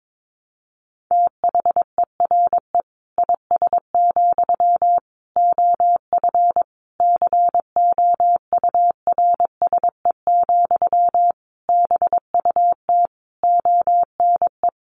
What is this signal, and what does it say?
Morse code, heard as a keyed tone, about 22 wpm, 700 Hz. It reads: THERE IS, OF COURSE, BUT ONE
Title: Sentences from Top 1200 Words 22wpm